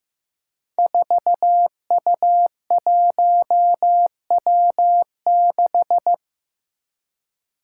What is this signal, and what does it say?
Morse code, heard as a keyed tone, about 15 wpm, 700 Hz. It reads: 4U1W6
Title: Five Challenging Letter-Number Combinations 15wpm